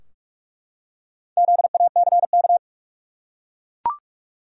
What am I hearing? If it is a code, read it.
Morse code, heard as a keyed tone, about 44 wpm, 700 Hz. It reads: ZACK
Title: USA Common Names - Max 4 Letters - 45wpm